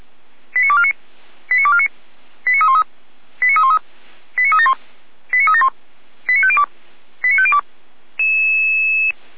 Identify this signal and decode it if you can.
Five-tone selective call (ZVEI), heard as quick runs of five tones, tones of about 70 ms. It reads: 89249 89249 89313 89313 89371 89371 89502 89502